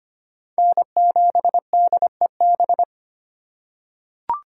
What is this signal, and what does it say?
Morse code, heard as a keyed tone, about 25 wpm, 700 Hz. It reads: N7DEB
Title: USA Call Signs - 5 Characters - 2x Character Spacing - 25wpm